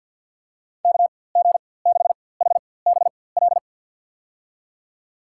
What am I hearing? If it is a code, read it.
Morse code, heard as a keyed tone, about 51 wpm, spaced as for 23 wpm, 700 Hz. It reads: KG6HBL
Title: USA Call Signs - 4x Character Spacing - 50wpm